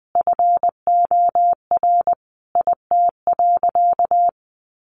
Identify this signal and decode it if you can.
Morse code, heard as a keyed tone, about 20 wpm, 700 Hz. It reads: FOR IT.
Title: Sentences from Top 300 Words 20wpm